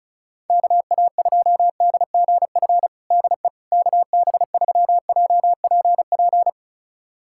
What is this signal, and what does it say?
Morse code, heard as a keyed tone, about 36 wpm, 700 Hz. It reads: KA2DGF DE KB3JPP
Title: USA Callsign DE Callsign - 6 Characters - 35wpm